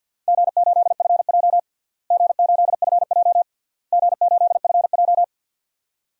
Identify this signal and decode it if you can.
Morse code, heard as a keyed tone, about 51 wpm, 700 Hz. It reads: G8FJ G8FJ G8FJ
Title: DXCC-100 Call Signs - 4 Characters - Repeated 3x - 50wpm